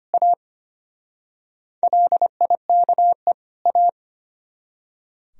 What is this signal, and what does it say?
Morse code, heard as a keyed tone, about 32 wpm, 700 Hz. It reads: A LIKE A